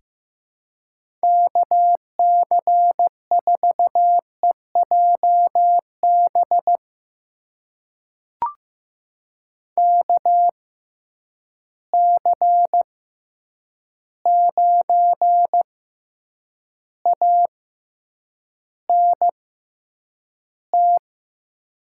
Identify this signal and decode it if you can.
Morse code, heard as a keyed tone, about 15 wpm, spaced as for 8 wpm, 700 Hz. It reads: KC4EJB KC9ANT
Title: USA Call Signs - 6x Character Spacing - 15wpm